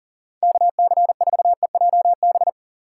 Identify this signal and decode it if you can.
Morse code, heard as a keyed tone, about 40 wpm, 700 Hz. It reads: KC4EJB
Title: USA Call Signs - 6 Characters - 3x Character Spacing - 40wpm